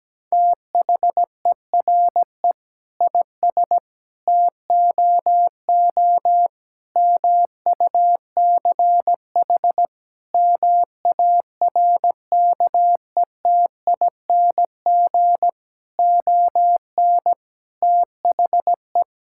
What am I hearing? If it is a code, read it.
Morse code, heard as a keyed tone, about 17 wpm, 700 Hz. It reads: THERE IS TOO MUCH MARKETING ON THE
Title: Sentences from Top 1000 Words - Review Part 5 - 17wpm